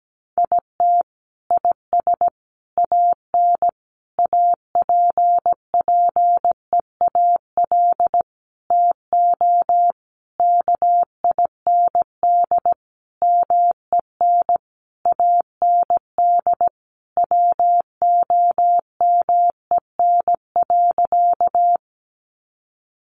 Morse code, heard as a keyed tone, about 17 wpm, 700 Hz. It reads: IT IS AN APPEAL TO KIND MEN AND WOMEN.
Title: Sentences from Top 1300 Words 17wpm